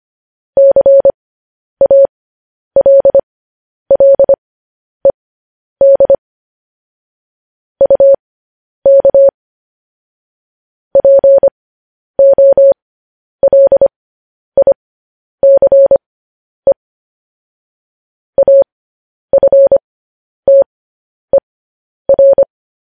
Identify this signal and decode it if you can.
Morse code, heard as a keyed tone, about 25 wpm, spaced as for 10 wpm, 550 Hz. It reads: CALLED UK POLICE AFTER